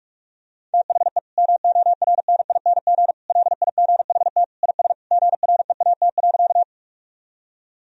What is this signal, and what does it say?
Morse code, heard as a keyed tone, about 46 wpm, 700 Hz. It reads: THE MORNING LIGHT IS GREAT.
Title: Sentences from Top 1000 Words - Review Part 1 - 45wpm